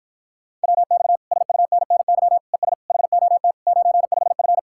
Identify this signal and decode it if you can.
Morse code, heard as a keyed tone, about 54 wpm, 700 Hz. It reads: WX SUNNY ES HOT 95F